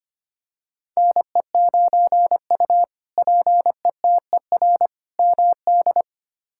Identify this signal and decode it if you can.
Morse code, heard as a keyed tone, about 25 wpm, 700 Hz. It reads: NE9U PETER MD
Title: K1USN SST QSO Practice - 25wpm